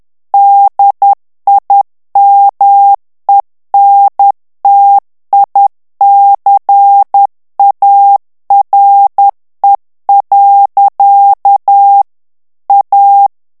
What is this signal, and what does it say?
Morse code, heard as a keyed tone, about 10 wpm, 800 Hz. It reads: DIMENTICARE. A
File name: morse mp3 / ITALIEN texte clair 1 vitesse 10 mots minute